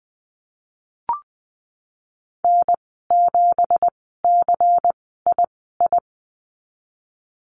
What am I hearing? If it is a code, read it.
Morse code, heard as a keyed tone, about 20 wpm, spaced as for 15 wpm, 700 Hz. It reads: N7CII
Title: 179 Lesson 29g - Slash - Call Signs 20wpm